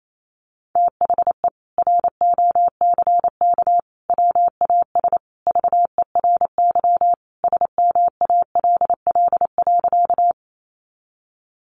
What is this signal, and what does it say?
Morse code, heard as a keyed tone, about 28 wpm, 700 Hz. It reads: THE ROCK WAS VERY SMALL.
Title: Sentences from Top 400 Words 28wpm